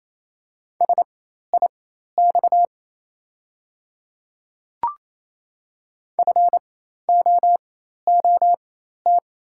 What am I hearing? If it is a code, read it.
Morse code, heard as a keyed tone, about 28 wpm, spaced as for 13 wpm, 700 Hz. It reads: SIX FOOT
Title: 4x Character Spacing - Top 500 Words - 28wpm